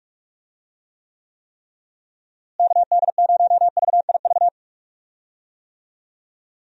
Morse code, heard as a keyed tone, about 41 wpm, 700 Hz. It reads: KD0VIV
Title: USA Call Signs - 6 Characters - Repeated 3x - 45wpm